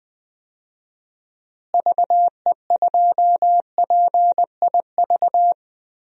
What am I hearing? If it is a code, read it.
Morse code, heard as a keyed tone, about 20 wpm, 700 Hz. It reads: VE2PIV
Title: DXCC-100 Call Signs - 6 Characters - Repeated 3x - 20wpm